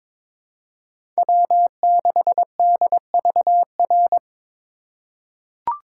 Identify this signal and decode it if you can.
Morse code, heard as a keyed tone, about 23 wpm, 700 Hz. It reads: W6DVR
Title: USA Call Signs - 4x Character Spacing - 22wpm